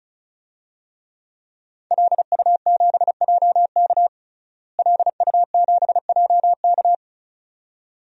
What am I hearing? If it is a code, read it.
Morse code, heard as a keyed tone, about 34 wpm, 700 Hz. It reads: LU7JK LU7JK
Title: DXCC-100 Call Signs - 5 Characters - Repeated 2x - 35wpm